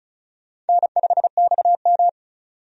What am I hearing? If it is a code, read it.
Morse code, heard as a keyed tone, about 36 wpm, 700 Hz. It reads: N5XM
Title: USA Call Signs - 4 Characters - 35wpm